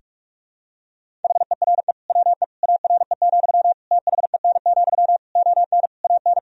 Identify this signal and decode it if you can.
Morse code, heard as a keyed tone, about 46 wpm, 700 Hz. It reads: HERE WE ARE, THEN, ON AN
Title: Sentences from Top 1000 Words - Review Part 3 - 45wpm